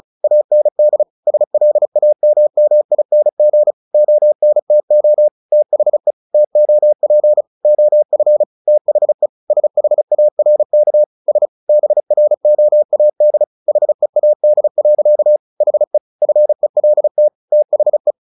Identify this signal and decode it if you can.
Morse code, heard as a keyed tone, about 34 wpm, 600 Hz. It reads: AND SLAMMING ONTO THE TOP OF THE SHARK S BROAD HEAD. HE FELT THE
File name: the old man and the sea 2 35wpm